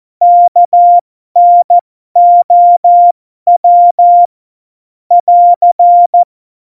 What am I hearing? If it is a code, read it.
Morse code, heard as a keyed tone, about 14 wpm, spaced as for 12 wpm, 700 Hz. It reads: KNOW <AR>
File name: Morsekurs / Witze